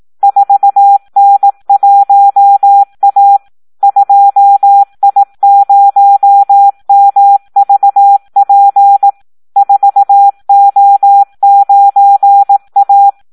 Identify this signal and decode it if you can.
Morse code, heard as a keyed tone, about 18 wpm, 800 Hz. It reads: 4N1A 2I0MVP 4O9A
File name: Audio practica CW / SD 1 18